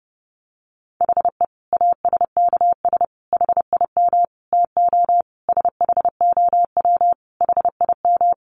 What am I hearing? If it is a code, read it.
Morse code, heard as a keyed tone, about 30 wpm, 700 Hz. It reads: HE ASKS HIM TO SHOW HIM
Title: Easy Phrases 30wpm